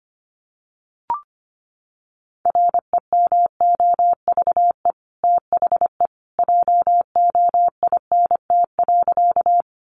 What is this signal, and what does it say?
Morse code, heard as a keyed tone, about 25 wpm, 700 Hz. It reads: REMOVE THE JOINT.
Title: Sentences from Top 1300 Words 25wpm